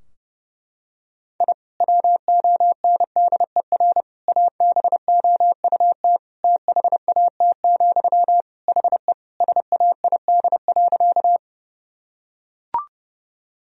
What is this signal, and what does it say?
Morse code, heard as a keyed tone, about 32 wpm, 700 Hz. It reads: I WONDER ABOUT THAT, HE SAID.
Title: Sentences from Top 1000 Words - Review Part 3 - 30wpm